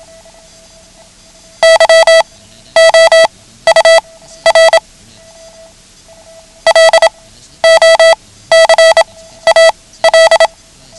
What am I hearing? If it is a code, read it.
Morse code, heard as a keyed tone, about 27 wpm, spaced as for 14 wpm, 660 Hz. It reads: YOUR LOCAL
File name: ham radio / CodeQuick / Code Quick 09